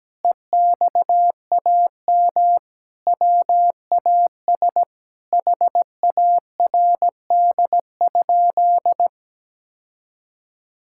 Morse code, heard as a keyed tone, about 17 wpm, 700 Hz. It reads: EXAM WAS HARD?
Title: Sentences from Top 1000 Words - Review Part 5 - 17wpm